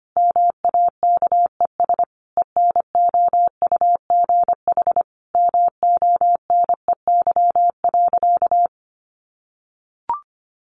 Morse code, heard as a keyed tone, about 25 wpm, 700 Hz. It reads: MAKES ENOUGH MONEY.
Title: Additional Easy Phrases 25wpm